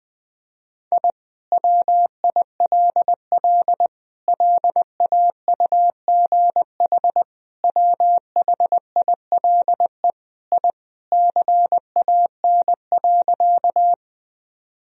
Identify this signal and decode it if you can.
Morse code, heard as a keyed tone, about 20 wpm, 700 Hz. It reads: I WILL LAUGH WHILE I CAN.
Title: Sentences from Top 1000 Words - Review Part 1 - 20wpm